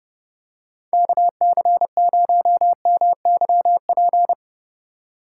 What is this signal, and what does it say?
Morse code, heard as a keyed tone, about 29 wpm, 700 Hz. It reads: KC0MYP